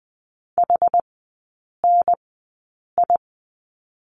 Morse code, heard as a keyed tone, about 20 wpm, spaced as for 8 wpm, 700 Hz. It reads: HNI
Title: 074 Lesson 13e - Letter L - Three Characters 20wpm